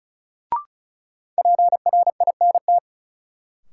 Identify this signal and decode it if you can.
Morse code, heard as a keyed tone, about 36 wpm, 700 Hz. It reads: PRINT
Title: Sentences from Top 1400 Words - New Word Drill Alt - 35wpm